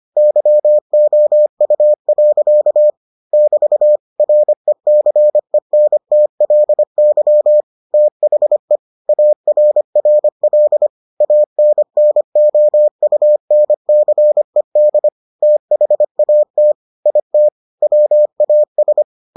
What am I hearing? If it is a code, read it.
Morse code, heard as a keyed tone, about 25 wpm, 600 Hz. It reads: YOU. <BT> RECENTLY THE ARRL ANNOUNCED THAT IT WAS